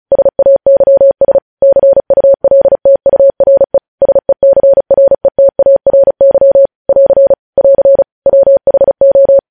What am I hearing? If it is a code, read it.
Morse code, heard as a keyed tone, about 36 wpm, 550 Hz. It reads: SAYS CULTURE SECRETARY <AR> <AR> WHO
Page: News headlines in Morse code at 35 wpm